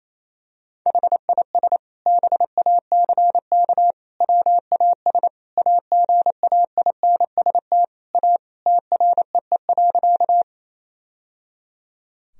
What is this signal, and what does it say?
Morse code, heard as a keyed tone, about 28 wpm, 700 Hz. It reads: HIS BACK WAS AGAINST A TREE.
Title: Sentences from Top 1000 Words - Review Part 1 - 28wpm